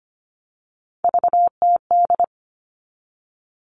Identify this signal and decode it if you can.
Morse code, heard as a keyed tone, about 25 wpm, 700 Hz. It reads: VTD
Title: All Cut Numbers - 3-Digit 25wpm